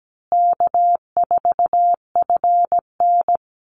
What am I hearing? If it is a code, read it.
Morse code, heard as a keyed tone, about 17 wpm, 700 Hz. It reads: K4FN
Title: Call Signs Repeated 2x - 17wpm